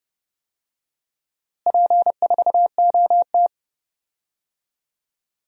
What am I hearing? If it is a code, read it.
Morse code, heard as a keyed tone, about 30 wpm, 700 Hz. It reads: P4OT
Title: DXCC-100 Call Signs - 4 Characters - Repeated 2x - 30wpm